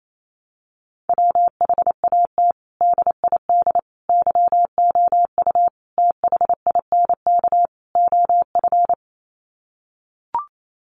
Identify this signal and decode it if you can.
Morse code, heard as a keyed tone, about 28 wpm, 700 Hz. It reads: WHAT DID YOU THINK OF
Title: Sets of 5 Words Encore2 28wpm